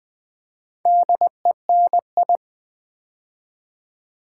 Morse code, heard as a keyed tone, about 20 wpm, 700 Hz. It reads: DENI
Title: Build A Word - 2 Letters At A Time - 6-Letter Words - 20wpm